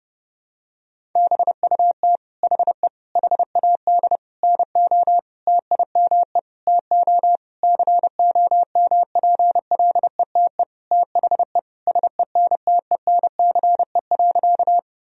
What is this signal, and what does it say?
Morse code, heard as a keyed tone, about 30 wpm, 700 Hz. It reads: BUT HE HAD NO TIME TO COMPLETE THE SENTENCE.
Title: Sentences from Top 1000 Words - Review Part 1 - 30wpm